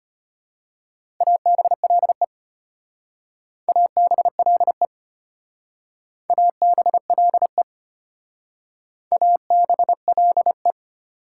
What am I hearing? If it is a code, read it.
Morse code, heard as a keyed tone, about 31 wpm, 690 Hz. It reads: ABLE ABLE ABLE ABLE
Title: Speed-Racing Prefixes and Suffixes 25wpm